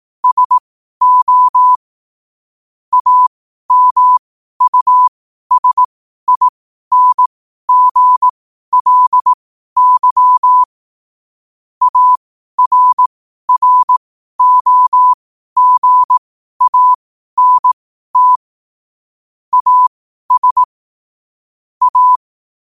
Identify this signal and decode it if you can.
Morse code, heard as a keyed tone, about 18 wpm, spaced as for 12 wpm, 1000 Hz. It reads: SO AMUSINGLY ARROGANT AS A
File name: qotdfiles / 12wpm